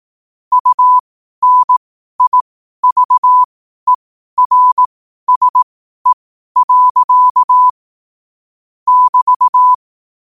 Morse code, heard as a keyed tone, about 18 wpm, spaced as for 12 wpm, 1000 Hz. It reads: UNIVERSE. <BT>
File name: qotdfiles / 12wpm